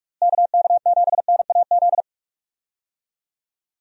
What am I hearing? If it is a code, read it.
Morse code, heard as a keyed tone, about 47 wpm, 700 Hz. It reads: KK7NAZ
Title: USA Call Signs - 6 Characters - Repeated 3x - 45wpm